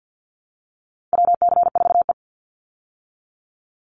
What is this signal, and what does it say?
Morse code, heard as a keyed tone, about 50 wpm, 700 Hz. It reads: WC4E
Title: Call Signs - 50wpm